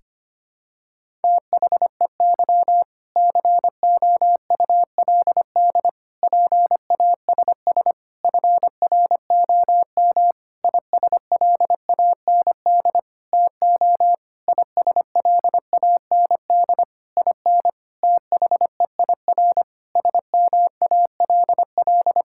Morse code, heard as a keyed tone, about 25 wpm, 700 Hz. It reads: THEY COULD PASS FROM ISLAND TO ISLAND IN THEIR SMALL
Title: Sentences from Top 1000 Words - Review Part 6 - 25wpm